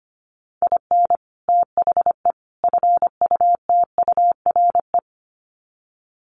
Morse code, heard as a keyed tone, about 25 wpm, 700 Hz. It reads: IN THE FUTURE
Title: Sets of 3 Words 25wpm